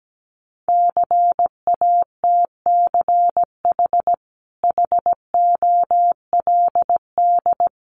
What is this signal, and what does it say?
Morse code, heard as a keyed tone, about 17 wpm, 700 Hz. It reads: CATCH HOLD
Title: Sentences from Top 1500 Words 17wpm